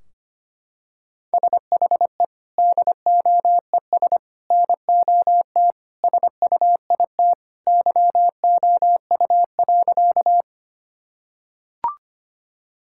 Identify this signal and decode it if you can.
Morse code, heard as a keyed tone, about 25 wpm, 700 Hz. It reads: SHE DOES NOT SUIT YOU.
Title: Sentences from Top 1600 Words 25wpm